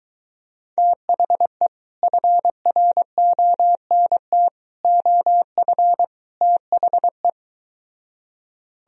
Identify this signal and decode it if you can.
Morse code, heard as a keyed tone, about 23 wpm, 700 Hz. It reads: THE FRONT OF THE